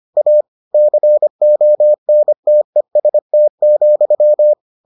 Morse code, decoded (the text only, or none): A CONTEST,